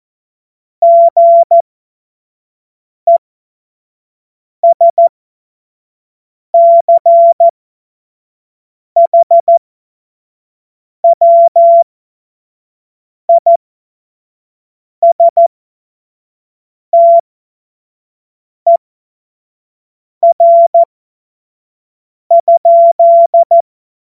Morse code, heard as a keyed tone, about 14 wpm, spaced as for 5 wpm, 680 Hz. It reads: GESCHWISTER?
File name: Morsekurs / Witze / Witz 28.November 2023 25 F70